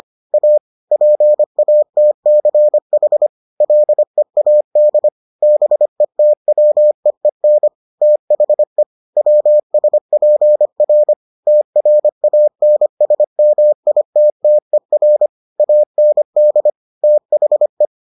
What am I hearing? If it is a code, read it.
Morse code, heard as a keyed tone, about 25 wpm, 600 Hz. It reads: A PATCH LEAD BETWEEN THE WSPR TRANSMITTER AND THE